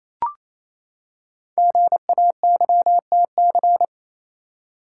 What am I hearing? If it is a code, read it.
Morse code, heard as a keyed tone, about 29 wpm, 700 Hz. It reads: GAYTC
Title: Code Groups - 5 Letters - 28wpm